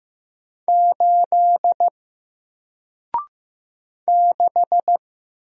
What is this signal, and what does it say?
Morse code, heard as a keyed tone, about 15 wpm, 700 Hz. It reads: 8 6
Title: Single Letter-Number Alt 15wpm